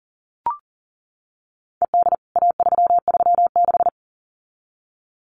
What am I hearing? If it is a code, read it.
Morse code, heard as a keyed tone, about 40 wpm, 700 Hz. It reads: ED A336
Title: CWT Exchange 40wpm